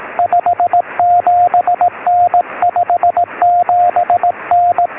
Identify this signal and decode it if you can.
Morse code, heard as a keyed tone, about 18 wpm, 710 Hz, with radio noise. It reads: 57N 57N
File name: sm7lcb / 03 sk7mw 13cm